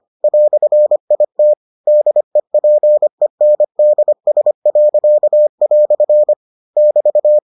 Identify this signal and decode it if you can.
Morse code, heard as a keyed tone, about 25 wpm, 600 Hz. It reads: "IT DEPENDS." <BT>